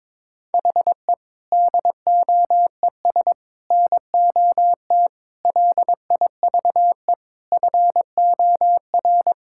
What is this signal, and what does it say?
Morse code, heard as a keyed tone, about 22 wpm, 700 Hz. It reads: HE DOES NOT LIVE FOR
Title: Sentences from Top 1600 Words 22wpm